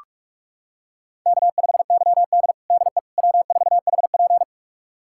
Morse code, heard as a keyed tone, about 46 wpm, 700 Hz. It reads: K5YD DE W4HP
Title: USA Callsign DE Callsign - 4 Characters - 45wpm